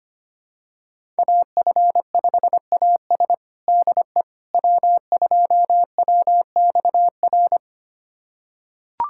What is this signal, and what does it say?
Morse code, heard as a keyed tone, about 25 wpm, 700 Hz. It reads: AF5AS DE W2WXR
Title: USA Callsign DE Callsign - 5 Characters - 25wpm